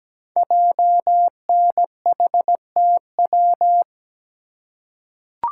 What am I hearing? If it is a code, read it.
Morse code, heard as a keyed tone, about 17 wpm, 700 Hz. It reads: JNHTW
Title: Code Groups - 5 Letters - 17wpm